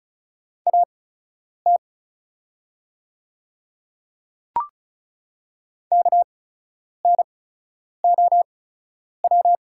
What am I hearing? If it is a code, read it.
Morse code, heard as a keyed tone, about 36 wpm, spaced as for 10 wpm, 700 Hz. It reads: AT KNOW
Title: 8x Character Spacing - Top 100 Words - 35wpm